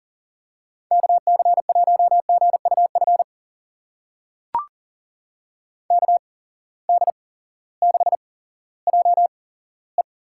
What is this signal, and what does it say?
Morse code, heard as a keyed tone, about 39 wpm, spaced as for 18 wpm, 700 Hz. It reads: KC1GUF KD6JE
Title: USA Call Signs - 8x Character Spacing - 40wpm